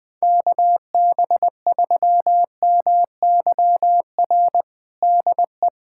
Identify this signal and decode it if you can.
Morse code, heard as a keyed tone, about 21 wpm, 700 Hz. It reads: KB3MYR DE
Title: USA Callsign DE Callsign - 6 Characters - 20wpm